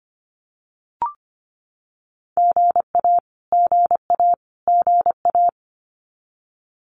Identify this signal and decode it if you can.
Morse code, heard as a keyed tone, about 25 wpm, 700 Hz. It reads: GA GA GA
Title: Abbreviations Repeated 3x - 25wpm